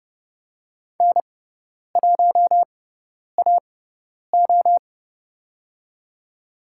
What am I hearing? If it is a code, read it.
Morse code, heard as a keyed tone, about 31 wpm, spaced as for 10 wpm, 700 Hz. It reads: N1AO
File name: Lesson 17g - Letter u - Call Signs 30wpm